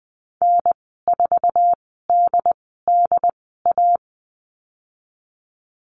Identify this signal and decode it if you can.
Morse code, heard as a keyed tone, about 19 wpm, spaced as for 14 wpm, 700 Hz. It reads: N4DDA
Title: 161 Lesson 26g - Letter g - Call Signs 20wpm